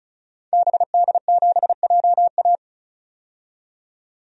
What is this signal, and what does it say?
Morse code, heard as a keyed tone, about 34 wpm, 700 Hz. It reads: BD7JA
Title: DXCC-100 Call Signs - 5 Characters - 35wpm